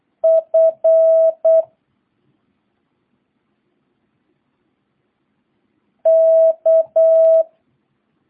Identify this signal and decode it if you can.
Morse code, heard as a keyed tone, about 8 wpm, 650 Hz. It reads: F K